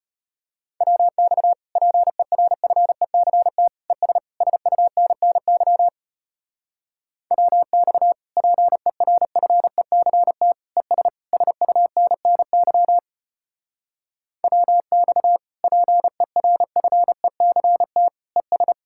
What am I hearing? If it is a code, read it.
Morse code, heard as a keyed tone, about 33 wpm, 700 Hz. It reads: WX PERFECT ES SUNNY WX PERFECT ES SUNNY WX PERFECT ES